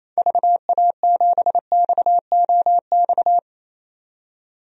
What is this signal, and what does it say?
Morse code, heard as a keyed tone, about 28 wpm, 700 Hz. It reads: VA7XOX
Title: DXCC-100 Call Signs - 6 Characters - 28wpm